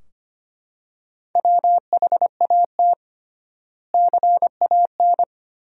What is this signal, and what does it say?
Morse code, heard as a keyed tone, about 25 wpm, 700 Hz. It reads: WHAT CAN
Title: 3x Word-Spacing - Sentences from Top 100 Words - 25wpm